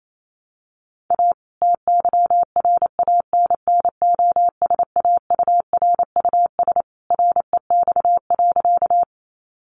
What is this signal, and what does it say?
Morse code, heard as a keyed tone, about 28 wpm, 700 Hz. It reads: A TYRANNOSAURUS REX.
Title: A Sound Of Thunder 28wpm